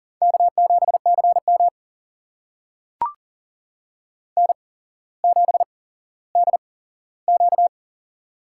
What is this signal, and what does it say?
Morse code, heard as a keyed tone, about 40 wpm, spaced as for 17 wpm, 700 Hz. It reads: K7CM N7DQ
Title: USA Call Signs - 4 Characters - 8x Character Spacing - 40wpm